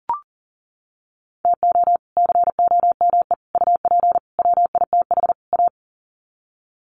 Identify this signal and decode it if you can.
Morse code, heard as a keyed tone, about 40 wpm, 700 Hz. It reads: TO COME UP WITH A
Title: Sets of 5 Words 40wpm